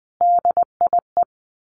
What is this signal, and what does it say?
Morse code, heard as a keyed tone, about 20 wpm, 700 Hz. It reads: DIE
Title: A Sound Of Thunder 20wpm